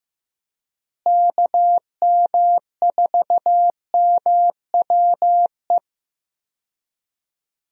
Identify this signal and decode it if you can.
Morse code, heard as a keyed tone, about 16 wpm, 700 Hz. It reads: KM4MWE
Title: USA Call Signs - 6 Characters - Repeated 3x - 15wpm